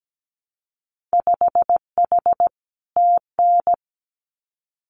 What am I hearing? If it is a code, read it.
Morse code, heard as a keyed tone, about 17 wpm, 700 Hz. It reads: 5H TN
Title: Winter Field Day Exchanges Encore 17wpm